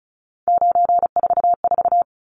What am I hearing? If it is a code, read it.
Morse code, heard as a keyed tone, about 34 wpm, 700 Hz. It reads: 944
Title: Numbers - 3-Digit 35wpm